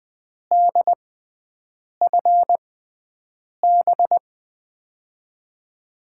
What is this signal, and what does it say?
Morse code, heard as a keyed tone, about 21 wpm, spaced as for 7 wpm, 700 Hz. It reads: DFB
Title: USA Call Signs - 6 Characters - 6x Character Spacing - 20wpm